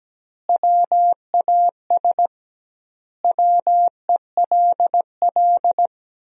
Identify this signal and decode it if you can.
Morse code, heard as a keyed tone, about 17 wpm, 700 Hz. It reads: WAS WELL
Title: 2x Word-Spacing - Sentences from Top 400 Words - 17wpm